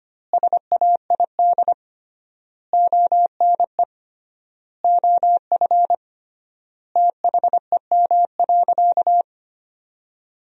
Morse code, heard as a keyed tone, about 25 wpm, 700 Hz. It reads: SAID ONE OF THEM.
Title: 3x Word-Spacing - Sentences from Top 300 Words - 25wpm